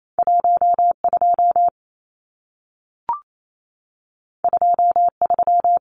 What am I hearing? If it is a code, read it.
Morse code, heard as a keyed tone, about 28 wpm, 700 Hz. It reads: 12 23
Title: Numbers - 2-Digit 28wpm